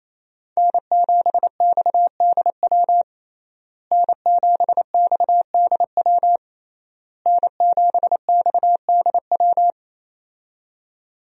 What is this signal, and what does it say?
Morse code, heard as a keyed tone, about 28 wpm, 700 Hz. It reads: N7XDW N7XDW N7XDW
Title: USA Call Signs - 5 Characters - Repeated 3x - 28wpm